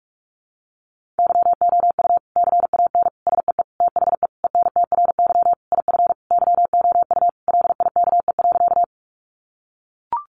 Morse code, heard as a keyed tone, about 44 wpm, 700 Hz. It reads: YOU CAN SEE THE ENTRY IF YOU LIKE.
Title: Sentences from Top 1500 Words 45wpm